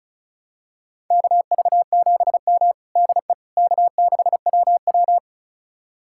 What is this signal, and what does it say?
Morse code, heard as a keyed tone, about 34 wpm, 700 Hz. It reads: KV7M DE K6WW
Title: USA Callsign DE Callsign - 4 Characters - 35wpm